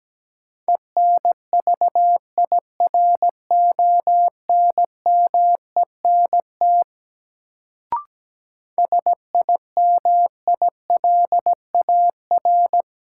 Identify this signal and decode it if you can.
Morse code, heard as a keyed tone, about 17 wpm, 700 Hz. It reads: ENVIRONMENT SIMILAR
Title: Sentences from Top 1000 Words - Review Part 5 - New Word Drill - 17wpm